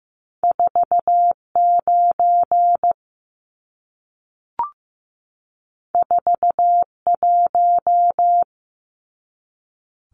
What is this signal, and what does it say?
Morse code, heard as a keyed tone, about 15 wpm, 700 Hz. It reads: 49 41
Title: Numbers - 2-Digit 15wpm